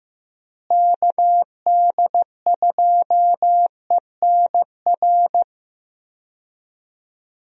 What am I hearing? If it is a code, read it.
Morse code, heard as a keyed tone, about 16 wpm, 700 Hz. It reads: KD2ENR
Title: USA Call Signs - 6 Characters - Repeated 2x - 15wpm